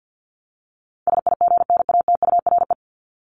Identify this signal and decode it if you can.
Morse code, heard as a keyed tone, about 50 wpm, 700 Hz. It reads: SIGNATURE
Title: Top 1751-2000 Words 50wpm